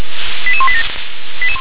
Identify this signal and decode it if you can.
Five-tone selective call (ZVEI), heard as quick runs of five tones, tones of about 70 ms. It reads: 88187
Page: SDR/ZVEI